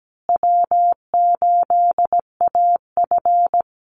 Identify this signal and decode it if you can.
Morse code, heard as a keyed tone, about 17 wpm, 700 Hz. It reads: W8AF
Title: USA Call Signs Repeated 2x 17wpm